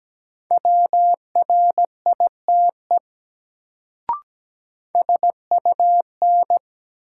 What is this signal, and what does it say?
Morse code, heard as a keyed tone, about 17 wpm, 700 Hz. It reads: WRITE SUN
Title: Sentences from Top 300 Words - New Word Drill - 17wpm